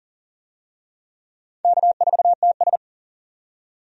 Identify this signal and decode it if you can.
Morse code, heard as a keyed tone, about 40 wpm, 700 Hz. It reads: K4TS
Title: USA Call Signs - 4 Characters - 40wpm